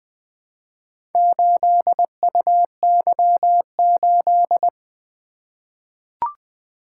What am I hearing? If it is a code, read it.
Morse code, heard as a keyed tone, about 20 wpm, 700 Hz. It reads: 8UY8
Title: Four Letter-Number Combinations - 20wpm